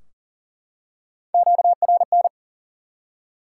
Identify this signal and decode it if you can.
Morse code, heard as a keyed tone, about 40 wpm, 700 Hz. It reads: QRN
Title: Top 180 QSO Elements 40wpm